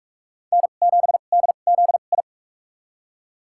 Morse code, heard as a keyed tone, about 46 wpm, spaced as for 33 wpm, 690 Hz. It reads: N7DZI
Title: USA Call Signs - 5 Characters - 2x Character Spacing - 45wpm